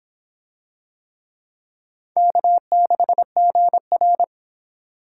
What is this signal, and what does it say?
Morse code, heard as a keyed tone, about 24 wpm, 700 Hz. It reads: K6GR